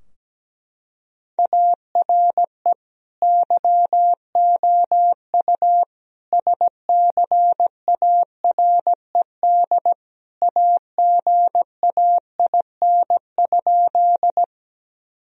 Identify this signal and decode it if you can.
Morse code, heard as a keyed tone, about 17 wpm, 700 Hz. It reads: ARE YOU SCARED AGAIN?
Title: Sentences from Top 1000 Words - Review Part 4 - 17wpm